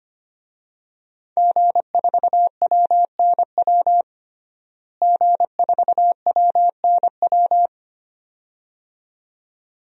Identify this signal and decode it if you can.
Morse code, heard as a keyed tone, about 25 wpm, 700 Hz. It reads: G4WNW G4WNW
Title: DXCC-100 Call Signs - 5 Characters - Repeated 2x - 25wpm